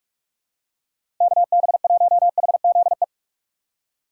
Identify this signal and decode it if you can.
Morse code, heard as a keyed tone, about 46 wpm, 700 Hz. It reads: KB1HZE
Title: USA Call Signs - 6 Characters - 4x Character Spacing - 45wpm